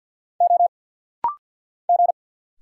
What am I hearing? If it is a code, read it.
Morse code, heard as a keyed tone, about 50 wpm, 700 Hz. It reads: O G
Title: Single Letter-Number Alt - Rapid-Fire 50wpm